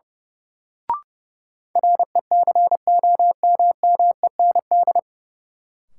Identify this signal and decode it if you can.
Morse code, heard as a keyed tone, about 30 wpm, 700 Hz. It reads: RECOMMEND
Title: Sentences from Top 1200 Words - New Word Drill Alt - 30wpm